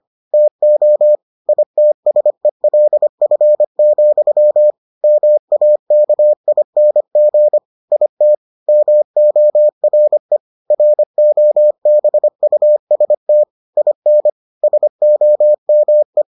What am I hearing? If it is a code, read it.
Morse code, heard as a keyed tone, about 25 wpm, 600 Hz. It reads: TO ITSELF, MAKING IT MORE ROBUST IN SOME